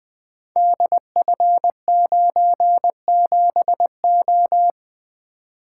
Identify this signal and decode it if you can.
Morse code, heard as a keyed tone, about 20 wpm, 700 Hz. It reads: DF97O
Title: Code Groups - 5 Letter:Numbers - 20wpm